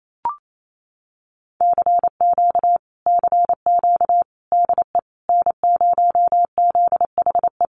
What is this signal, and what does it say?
Morse code, heard as a keyed tone, about 28 wpm, 700 Hz. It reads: CQ CQ DE N0ZHE
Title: Common QSO Phrases Encore 28wpm